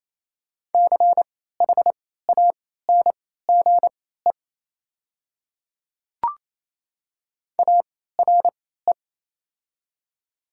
Morse code, heard as a keyed tone, about 27 wpm, spaced as for 16 wpm, 700 Hz. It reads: CHANGE ARE
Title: 3x Character Spacing - Top 500 Words - 28wpm